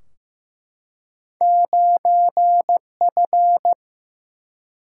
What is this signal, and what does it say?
Morse code, heard as a keyed tone, about 15 wpm, 700 Hz. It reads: 9F
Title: Single-Letter Single-Number 15wpm